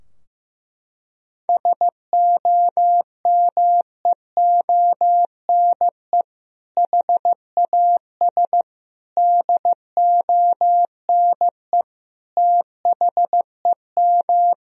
Morse code, heard as a keyed tone, about 16 wpm, 700 Hz. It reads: SOMEONE HAS DONE THEM
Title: Sentences from Top 1000 Words - Review Part 4 - 15wpm